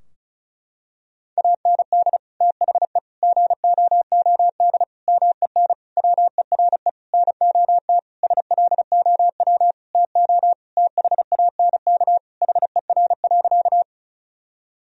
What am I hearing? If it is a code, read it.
Morse code, heard as a keyed tone, about 34 wpm, 700 Hz. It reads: AND THE GOOD MEN WERE NOT SLOW TO THANK HER.
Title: Sentences from Top 1000 Words - Review Part 2 - 35wpm